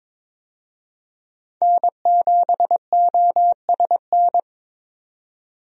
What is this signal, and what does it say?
Morse code, heard as a keyed tone, about 22 wpm, 700 Hz. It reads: N7OSN
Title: USA Call Signs - 5 Characters - 22wpm